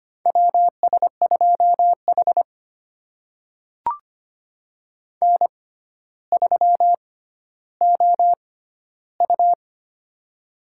Morse code, heard as a keyed tone, about 25 wpm, spaced as for 13 wpm, 700 Hz. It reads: WS2H N3OU
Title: USA Call Signs - 6x Character Spacing - 25wpm